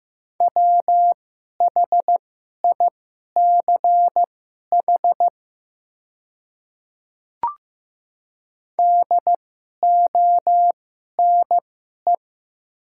Morse code, heard as a keyed tone, about 15 wpm, spaced as for 11 wpm, 690 Hz. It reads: WHICH DONE
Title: 2x Character Spacing - Top 500 Words - 15wpm